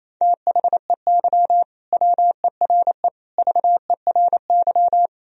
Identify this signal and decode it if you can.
Morse code, heard as a keyed tone, about 28 wpm, 700 Hz. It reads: THEY WERE VERY
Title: Sentences from Top 1000 Words - Review Part 3 - 28wpm